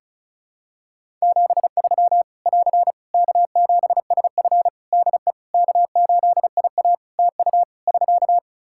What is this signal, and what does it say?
Morse code, heard as a keyed tone, about 36 wpm, 700 Hz. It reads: 73 <AR> K7SF DE K8IA TU <SK>
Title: Practice QSO - Exchange 4 - Ending - 35wpm